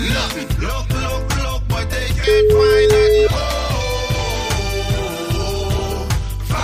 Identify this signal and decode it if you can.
Ringback tone (Europe, 425 Hz), with noise.